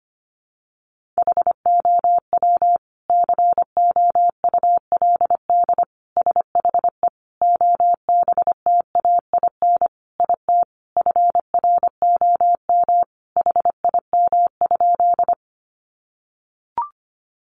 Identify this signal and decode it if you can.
Morse code, heard as a keyed tone, about 25 wpm, 700 Hz. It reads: HOW COULD SHE OBTAIN IT FROM HIM?
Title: Sentences from Top 1300 Words 25wpm